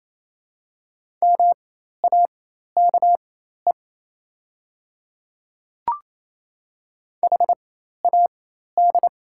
Morse code, heard as a keyed tone, about 28 wpm, spaced as for 13 wpm, 690 Hz. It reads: MAKE HAD
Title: 4x Character Spacing - Top 500 Words - 28wpm